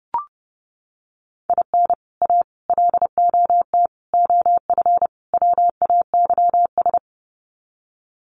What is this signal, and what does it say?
Morse code, heard as a keyed tone, about 30 wpm, 700 Hz. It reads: IN A LOT OF WAYS
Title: Sets of 5 Words Encore 30wpm